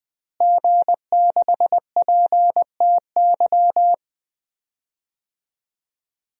Morse code, heard as a keyed tone, about 20 wpm, 700 Hz. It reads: G6PTY
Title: DXCC-100 Call Signs - 5 Characters - Repeated 2x - 20wpm